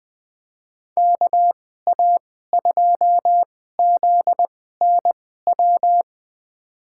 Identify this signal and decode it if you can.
Morse code, heard as a keyed tone, about 20 wpm, spaced as for 14 wpm, 700 Hz. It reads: KA2ZNW
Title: USA Call Signs - 6 Characters - 2x Character Spacing - 20wpm